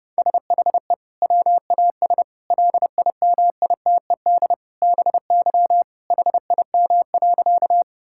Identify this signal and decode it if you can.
Morse code, heard as a keyed tone, about 28 wpm, 700 Hz. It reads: SHE WAS LIMITED BY HIM.
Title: Sentences from Top 1000 Words - Review Part 6 - 30wpm